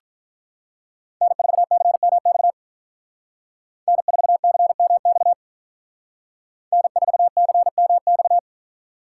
Morse code, heard as a keyed tone, about 42 wpm, 700 Hz. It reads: N4CMX N4CMX N4CMX
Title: Speed Racing - USA 1x3 Call Signs 35wpm